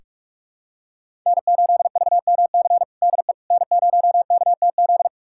Morse code, heard as a keyed tone, about 45 wpm, 700 Hz. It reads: N8VMC DE N0KTZ
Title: USA Callsign DE Callsign - 5 Characters - 45wpm